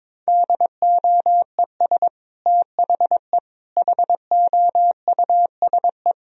Morse code, heard as a keyed tone, about 22 wpm, 700 Hz. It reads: DOES THE HOUSE
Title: Sentences from Top 1000 Words - Review Part 2 - 22wpm